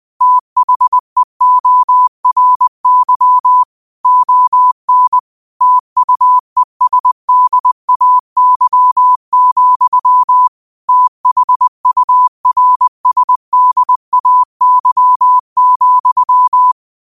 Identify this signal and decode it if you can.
Morse code, heard as a keyed tone, about 20 wpm, 1000 Hz. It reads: THEORY ON TUESDAY, THURSDAY,
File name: qotdfiles / 20wpm